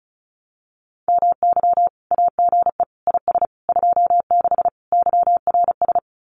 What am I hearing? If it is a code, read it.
Morse code, heard as a keyed tone, about 36 wpm, 700 Hz. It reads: MY AGE IS 26 YRS
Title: Common QSO Phrases Encore 35wpm